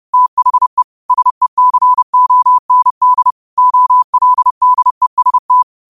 Morse code, heard as a keyed tone, about 30 wpm, 1000 Hz. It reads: THE SECOND OLDEST